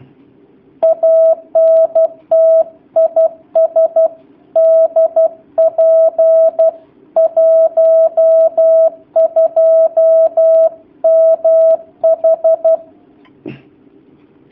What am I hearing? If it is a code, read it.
Morse code, heard as a keyed tone, about 12 wpm, 650 Hz. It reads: ANT IS DP 12MH
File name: jh8cbh / 2019cw / sp11- w